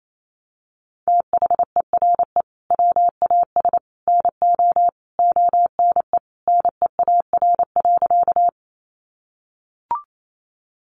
Morse code, heard as a keyed tone, about 28 wpm, 700 Hz. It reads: THERE WAS NO ONE NEAR.
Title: Sentences from Top 200 Words 28wpm